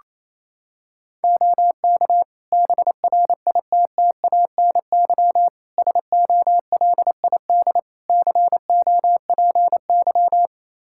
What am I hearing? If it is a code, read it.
Morse code, heard as a keyed tone, about 28 wpm, 700 Hz. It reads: OK BRITTANY SOLID COPY
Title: K1USN SST QSO Practice 28wpm